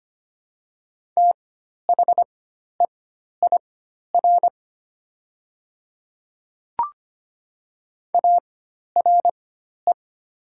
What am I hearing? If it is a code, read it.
Morse code, heard as a keyed tone, about 25 wpm, spaced as for 12 wpm, 690 Hz. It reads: THEIR ARE
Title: 4x Character Spacing - Top 100 Words - 25wpm